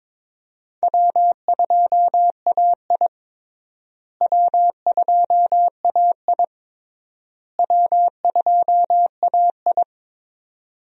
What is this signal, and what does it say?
Morse code, heard as a keyed tone, about 21 wpm, 700 Hz. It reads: W2AI W2AI W2AI
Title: USA Call Signs - 4 Characters - Repeated 3x - 22wpm